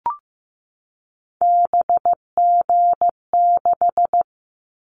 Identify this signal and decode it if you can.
Morse code, heard as a keyed tone, about 15 wpm, 700 Hz. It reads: BG6
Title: Callsign Prefixes for Top 100 DXCC - 15wpm